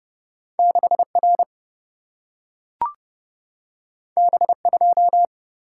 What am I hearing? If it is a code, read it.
Morse code, heard as a keyed tone, about 30 wpm, 700 Hz. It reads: 6 R B 2
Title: Single-Letter Single-Number 30wpm